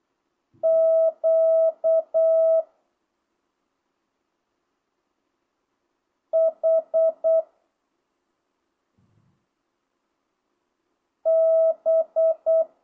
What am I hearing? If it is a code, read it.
Morse code, heard as a keyed tone, about 8 wpm, 650 Hz. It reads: Q H B
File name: jh8cbh / 2025cw